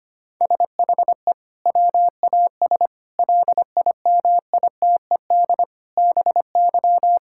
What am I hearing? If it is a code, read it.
Morse code, heard as a keyed tone, about 25 wpm, 700 Hz. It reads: SHE WAS LIMITED BY
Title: Sentences from Top 1000 Words - Review Part 6 - 25wpm